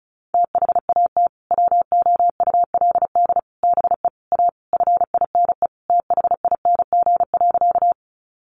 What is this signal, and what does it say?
Morse code, heard as a keyed tone, about 34 wpm, 700 Hz. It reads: THAT WOULD BE A FINE THING.
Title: Sentences from Top 500 Words 35wpm